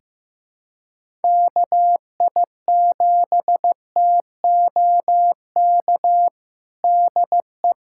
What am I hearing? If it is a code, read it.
Morse code, heard as a keyed tone, about 15 wpm, 700 Hz. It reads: KI7TOK DE
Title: USA Callsign DE Callsign - 6 Characters - 15wpm